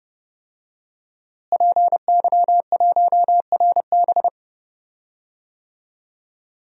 Morse code, heard as a keyed tone, about 32 wpm, 700 Hz. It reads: PY1RB
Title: DXCC-100 Call Signs - 5 Characters - Repeated 2x - 30wpm